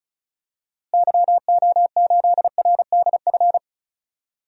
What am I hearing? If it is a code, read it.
Morse code, heard as a keyed tone, about 34 wpm, 700 Hz. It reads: YO8RDF
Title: DXCC-100 Call Signs - 6 Characters - Repeated 2x - 35wpm